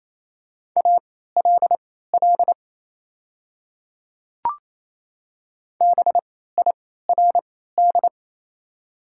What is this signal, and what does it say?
Morse code, heard as a keyed tone, about 27 wpm, spaced as for 16 wpm, 700 Hz. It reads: ALL BIRD
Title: 3x Character Spacing - Top 500 Words - 28wpm